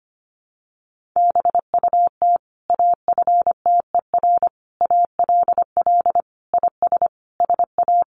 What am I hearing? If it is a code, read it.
Morse code, heard as a keyed tone, about 25 wpm, 700 Hz. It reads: BUT AFTER ALL IS SA
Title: Sentences from Top 1500 Words 25wpm